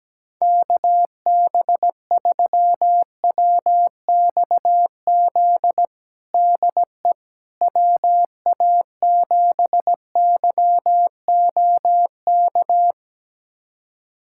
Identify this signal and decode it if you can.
Morse code, heard as a keyed tone, about 17 wpm, 700 Hz. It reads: KB3WXZ DE WA7YOK
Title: USA Callsign DE Callsign - 6 Characters - 17wpm